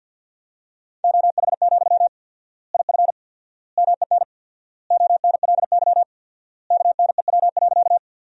Morse code, heard as a keyed tone, about 50 wpm, 700 Hz. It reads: OH, IF MEN ONLY KNEW.
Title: 4x Word-Spacing - Sentences from Top 500 Words - 50wpm